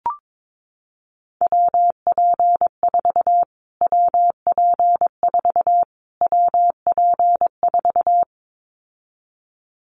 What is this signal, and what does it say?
Morse code, heard as a keyed tone, about 22 wpm, 700 Hz. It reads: WP4 WP4 WP4
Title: Callsign Prefixes for Top 25 DXCC - Repeated 3x - 22wpm